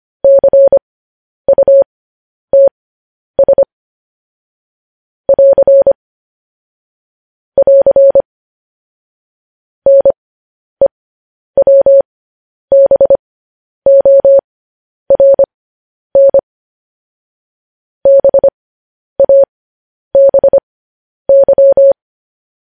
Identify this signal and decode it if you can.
Morse code, heard as a keyed tone, about 25 wpm, spaced as for 10 wpm, 550 Hz. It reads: CUTS <AR> <AR> NEWBORN BABY
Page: news-headlines-in-morse-code-at-10-wpm